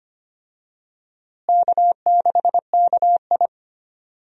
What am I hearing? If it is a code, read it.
Morse code, heard as a keyed tone, about 25 wpm, 700 Hz. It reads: K6KI
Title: USA Call Signs - 4 Characters - Repeated 3x - 25wpm